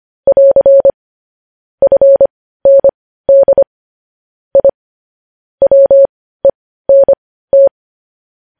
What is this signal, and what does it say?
Morse code, heard as a keyed tone, about 25 wpm, spaced as for 15 wpm, 550 Hz. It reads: <AR> FND I WENT
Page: news headlines in morse code at 15 wpm